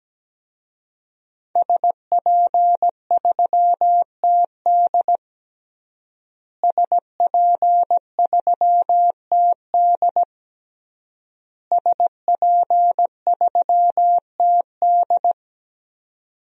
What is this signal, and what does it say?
Morse code, heard as a keyed tone, about 17 wpm, 700 Hz. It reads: SP3TD SP3TD SP3TD
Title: DXCC-100 Call Signs - 5 Characters - Repeated 3x - 17wpm